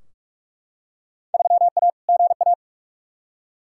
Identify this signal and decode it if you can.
Morse code, heard as a keyed tone, about 41 wpm, 700 Hz. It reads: 3A GA
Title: Field Day Exchanges - 45wpm